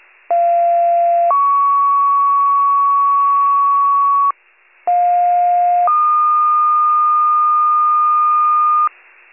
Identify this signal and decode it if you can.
Two-tone page, heard as a one-second tone, then a three-second tone, about 690 then 1120 Hz, about 710 then 1220 Hz. It is heard from 0.3 to 8.9 s.